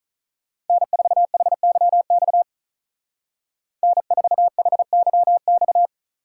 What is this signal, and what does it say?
Morse code, heard as a keyed tone, about 44 wpm, 700 Hz. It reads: N4HYX N4HYX